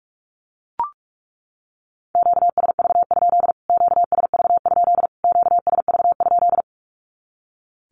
Morse code, heard as a keyed tone, about 44 wpm, 700 Hz. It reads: QSV? QSV? QSV?
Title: Q-Codes Repeated 3x 45wpm